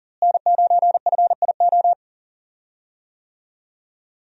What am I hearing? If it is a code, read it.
Morse code, heard as a keyed tone, about 37 wpm, 700 Hz. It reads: N9FIO
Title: USA Call Signs - 5 Characters - Repeated 3x - 40wpm